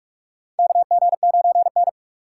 Morse code, heard as a keyed tone, about 46 wpm, 700 Hz. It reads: KG9N
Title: USA Call Signs - 3x Character Spacing - 45wpm